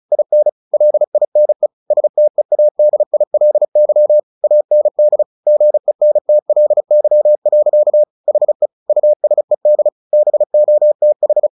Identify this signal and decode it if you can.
Morse code, heard as a keyed tone, about 34 wpm, 600 Hz. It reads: IN LINE STEADILY AND GENTLY. HE USED BOTH
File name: the old man and the sea 2 35wpm